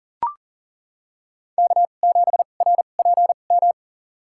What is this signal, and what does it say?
Morse code, heard as a keyed tone, about 40 wpm, spaced as for 29 wpm, 700 Hz. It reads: K7RPM
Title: USA Call Signs - 5 Characters - 2x Character Spacing - 40wpm